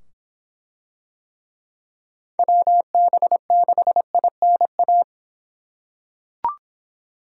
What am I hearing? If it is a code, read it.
Morse code, heard as a keyed tone, about 24 wpm, 700 Hz. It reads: WB6INA